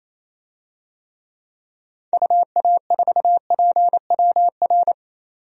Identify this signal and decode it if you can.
Morse code, heard as a keyed tone, about 28 wpm, 700 Hz. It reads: UA4PWR
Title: DXCC-100 Call Signs - 6 Characters - 28wpm